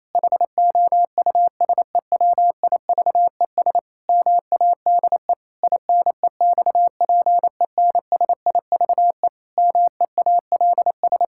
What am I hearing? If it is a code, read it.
Morse code, heard as a keyed tone, about 28 wpm, 700 Hz. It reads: HOUSEWIVES MADE INEXPENSIVE MEALS
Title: Pangram Challenge v2 - 28wpm